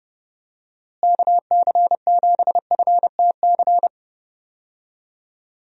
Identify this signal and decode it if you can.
Morse code, heard as a keyed tone, about 31 wpm, 700 Hz. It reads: KC7FTC
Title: USA Call Signs - 6 Characters - Repeated 3x - 30wpm